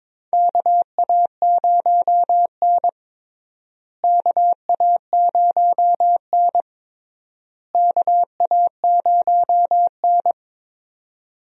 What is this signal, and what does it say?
Morse code, heard as a keyed tone, about 22 wpm, 700 Hz. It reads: KA0N KA0N KA0N
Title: USA Call Signs - 4 Characters - Repeated 3x - 22wpm